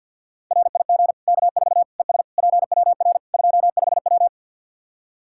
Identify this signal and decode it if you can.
Morse code, heard as a keyed tone, about 47 wpm, 700 Hz. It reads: RIG K4 ES PWR 25W
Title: Practice QSO - Exchange 2 - Info - 50wpm